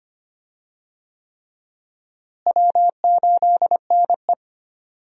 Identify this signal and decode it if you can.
Morse code, heard as a keyed tone, about 25 wpm, 700 Hz. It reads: W8NE
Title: USA Call Signs - 4 Characters - Repeated 3x - 25wpm